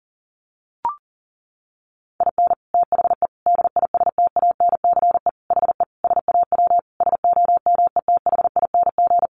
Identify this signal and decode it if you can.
Morse code, heard as a keyed tone, about 40 wpm, 700 Hz. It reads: IN THE DISTANCE HE SAW SOMETHING
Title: Sentences from Top 1200 Words 40wpm